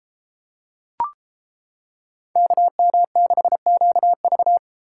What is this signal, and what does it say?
Morse code, heard as a keyed tone, about 29 wpm, 700 Hz. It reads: KM6QV